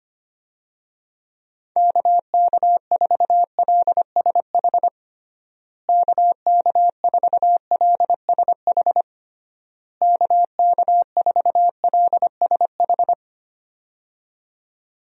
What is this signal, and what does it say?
Morse code, heard as a keyed tone, about 25 wpm, 700 Hz. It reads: KK4LSH KK4LSH KK4LSH
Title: USA Call Signs - 6 Characters - Repeated 3x - 25wpm